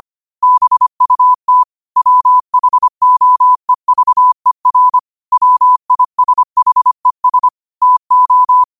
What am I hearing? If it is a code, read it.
Morse code, heard as a keyed tone, about 25 wpm, 1000 Hz. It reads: BUT WHOEVER WISHES TO
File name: qotdfiles / 25wpm